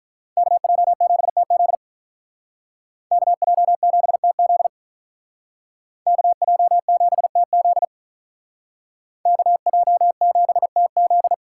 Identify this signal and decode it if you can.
Morse code, heard as a keyed tone, about 46 wpm, 700 Hz. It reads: KJ7TZ KJ7TZ KJ7TZ KJ7TZ